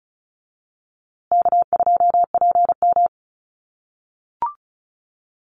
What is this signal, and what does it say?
Morse code, heard as a keyed tone, about 34 wpm, 700 Hz. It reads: K2PM
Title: USA Call Signs Repeated 3x 35wpm